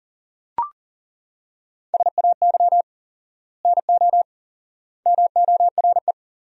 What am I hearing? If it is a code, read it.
Morse code, heard as a keyed tone, about 37 wpm, 700 Hz. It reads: SAY NO MORE
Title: 4x Word-Spacing - Sentences from Top 200 Words - 40wpm